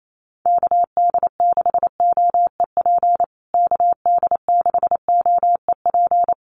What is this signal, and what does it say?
Morse code, heard as a keyed tone, about 28 wpm, 700 Hz. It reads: KD6OEP KD6OEP
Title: USA Call Signs Repeated 2x 28wpm